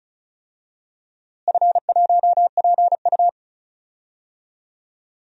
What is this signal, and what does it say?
Morse code, heard as a keyed tone, about 34 wpm, 700 Hz. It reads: F1PU
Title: DXCC-100 Call Signs - 4 Characters - Repeated 3x - 35wpm